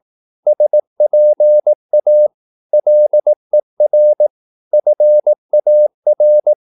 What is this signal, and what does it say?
Morse code, heard as a keyed tone, about 18 wpm, 600 Hz. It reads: SPA LER FAR
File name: Korte ord DK 18wpm